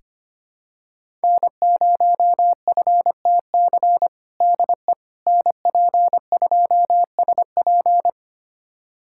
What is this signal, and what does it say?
Morse code, heard as a keyed tone, about 25 wpm, 700 Hz. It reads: N0FTC DE NP2SP
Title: USA Callsign DE Callsign - 5 Characters - 25wpm